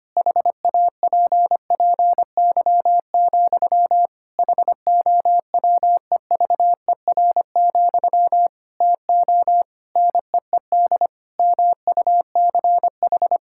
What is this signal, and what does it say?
Morse code, heard as a keyed tone, about 25 wpm, 700 Hz. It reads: HAPPY, HOWEVER, TO NEED MUCH
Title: Sentences from Top 1000 Words - Review Part 6 - 25wpm